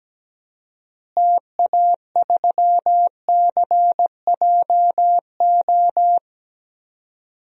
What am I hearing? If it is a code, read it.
Morse code, heard as a keyed tone, about 17 wpm, 700 Hz. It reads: TA3CJO
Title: DXCC-100 Call Signs - 6 Characters - Repeated 3x - 17wpm